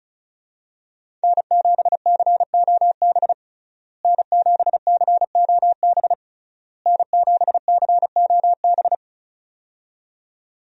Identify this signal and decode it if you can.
Morse code, heard as a keyed tone, about 36 wpm, 700 Hz. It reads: N7COB N7COB N7COB
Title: USA Call Signs - 5 Characters - Repeated 3x - 35wpm